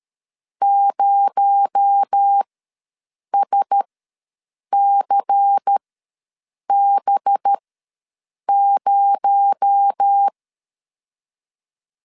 Morse code, heard as a keyed tone, about 13 wpm, spaced as for 7 wpm, 800 Hz. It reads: 0SCB0
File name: LEZIONI / 6 PKX0 / 6 PKX0 ws 12wpm 3448